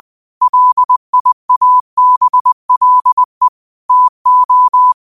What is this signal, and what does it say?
Morse code, heard as a keyed tone, about 20 wpm, 1000 Hz. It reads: LIABLE TO
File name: qotdfiles / 20wpm